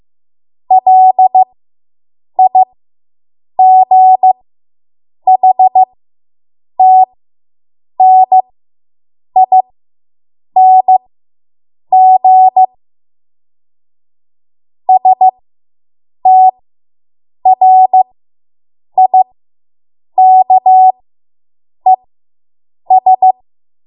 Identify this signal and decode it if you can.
Morse code, heard as a keyed tone, about 15 wpm, spaced as for 7 wpm, 750 Hz. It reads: LIGHTNING STRIKES